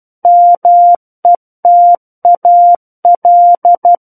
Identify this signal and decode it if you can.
Morse code, heard as a keyed tone, about 12 wpm, 700 Hz. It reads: METAL